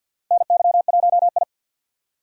Morse code, heard as a keyed tone, about 53 wpm, 700 Hz. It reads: NY1I
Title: USA Call Signs - 4 Characters - 50wpm